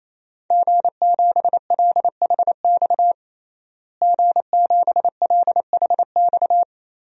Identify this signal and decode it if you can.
Morse code, heard as a keyed tone, about 28 wpm, 700 Hz. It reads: G7LHX G7LHX
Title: DXCC-100 Call Signs - 5 Characters - Repeated 3x - 28wpm